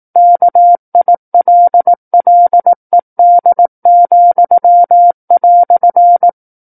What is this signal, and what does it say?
Morse code, heard as a keyed tone, about 18 wpm, 700 Hz. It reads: KILLED,"
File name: 18 WPM